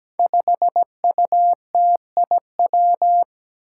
Morse code, heard as a keyed tone, about 17 wpm, 700 Hz. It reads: 5UTIW
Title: Five Challenging Letter-Number Combinations 17wpm